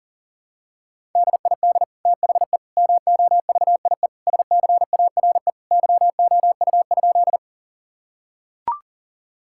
Morse code, heard as a keyed tone, about 40 wpm, 700 Hz. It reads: DID THE MOVIE SCARE YOU?
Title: Sentences from Top 1000 Words - Review Part 5 - 40wpm